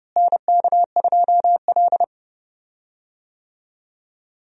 Morse code, heard as a keyed tone, about 30 wpm, 700 Hz. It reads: NK2L
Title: USA Call Signs - 4 Characters - Repeated 3x - 30wpm